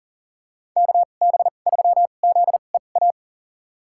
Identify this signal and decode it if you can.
Morse code, heard as a keyed tone, about 40 wpm, spaced as for 29 wpm, 700 Hz. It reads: KB3ZEA
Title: USA Call Signs - 6 Characters - 2x Character Spacing - 40wpm